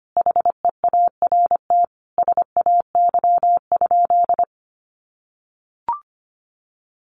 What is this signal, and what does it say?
Morse code, heard as a keyed tone, about 25 wpm, 700 Hz. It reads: HEART SAY?
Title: Sentences from Top 1200 Words 25wpm